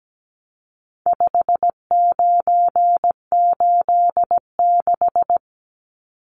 Morse code, heard as a keyed tone, about 17 wpm, 700 Hz. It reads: 5986
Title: Numbers - 4-Digit 17wpm